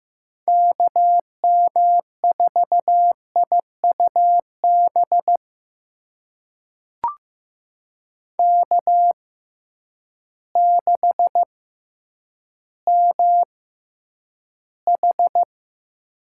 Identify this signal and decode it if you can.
Morse code, heard as a keyed tone, about 15 wpm, spaced as for 9 wpm, 700 Hz. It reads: KM4IUB K6MH
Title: USA Call Signs - 6x Character Spacing - 15wpm